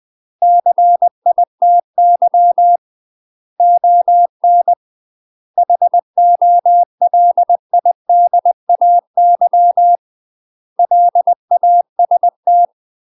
Morse code, heard as a keyed tone, about 20 wpm, 700 Hz. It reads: CITY ON HOLIDAY LAST
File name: UK MASTER QSOs TRAINING / 20WPM / UK MASTER QSOs 0005